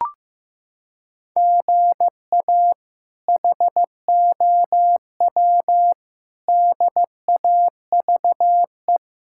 Morse code, heard as a keyed tone, about 15 wpm, 700 Hz. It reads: GA HOW DAVE
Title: K1USN SST QSO Practice - 15wpm